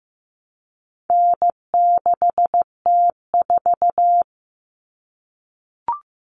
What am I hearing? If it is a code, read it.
Morse code, heard as a keyed tone, about 15 wpm, 700 Hz. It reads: N6T4
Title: All Cut Numbers - 4-Digit 15wpm